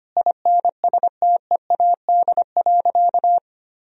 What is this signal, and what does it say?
Morse code, heard as a keyed tone, about 25 wpm, 700 Hz. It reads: INSTEAD.
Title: Sentences from Top 1000 Words - Review Part 4 - 25wpm